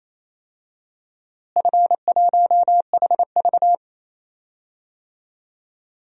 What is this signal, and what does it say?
Morse code, heard as a keyed tone, about 28 wpm, 700 Hz. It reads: F1HV
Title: DXCC-100 Call Signs - 4 Characters - Repeated 3x - 28wpm